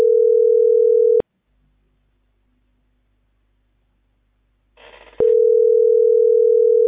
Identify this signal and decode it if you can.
Ringback tone (North America), 440 + 480 Hz.